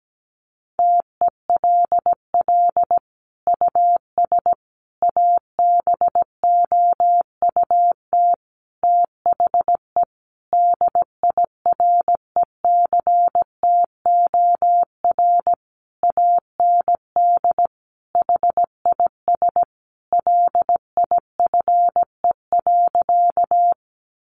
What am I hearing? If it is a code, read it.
Morse code, heard as a keyed tone, about 17 wpm, 700 Hz. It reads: TELL US ABOUT THE DIRECTOR AND HIS LIFE.
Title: Sentences from Top 1300 Words 17wpm